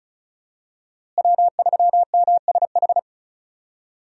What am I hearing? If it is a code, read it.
Morse code, heard as a keyed tone, about 36 wpm, 700 Hz. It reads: W3MSH
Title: USA Call Signs - 5 Characters - Repeated 3x - 35wpm